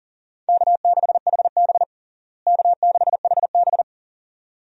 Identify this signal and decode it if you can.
Morse code, heard as a keyed tone, about 37 wpm, 700 Hz. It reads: K6HB K6HB
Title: USA Call Signs - 4 Characters - Repeated 2x - 40wpm